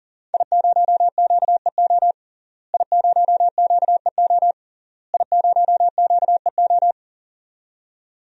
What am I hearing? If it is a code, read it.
Morse code, heard as a keyed tone, about 40 wpm, 700 Hz. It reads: I0QEO I0QEO I0QEO
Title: DXCC-100 Call Signs - 5 Characters - Repeated 3x - 40wpm